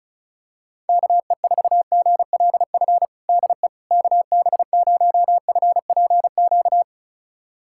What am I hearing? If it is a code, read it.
Morse code, heard as a keyed tone, about 36 wpm, 700 Hz. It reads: KE4GLF DE KB0FPQ
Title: USA Callsign DE Callsign 35wpm